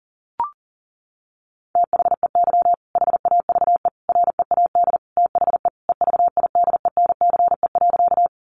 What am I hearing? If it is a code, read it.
Morse code, heard as a keyed tone, about 40 wpm, 700 Hz. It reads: THEY HAVE READ THE EVIDENCE.
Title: Sentences from Top 1300 Words 40wpm